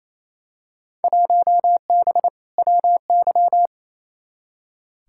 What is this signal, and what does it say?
Morse code, heard as a keyed tone, about 28 wpm, 700 Hz. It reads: 1B WY
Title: Field Day Exchanges Encore - 28wpm